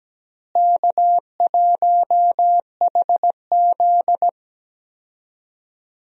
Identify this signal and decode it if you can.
Morse code, heard as a keyed tone, about 17 wpm, 700 Hz. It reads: K1HZ
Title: USA Call Signs - 4 Characters - Repeated 2x - 17wpm